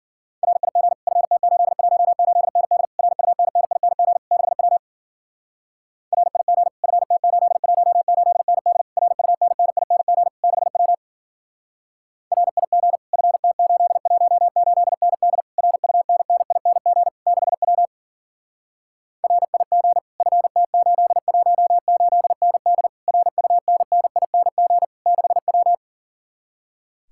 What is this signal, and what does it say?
Morse code, heard as a keyed tone, about 52 wpm, 700 Hz. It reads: RIG FT818ND RUNNING 6W RIG FT818ND RUNNING 6W RIG FT818ND RUNNING 6W RIG FT818ND RUNNING 6W